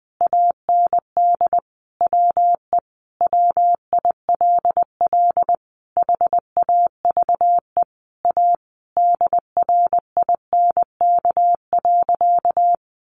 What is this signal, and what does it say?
Morse code, heard as a keyed tone, about 20 wpm, 700 Hz. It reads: AND WE WILL HAVE A DRINK.
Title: Sentences from Top 1500 Words 20wpm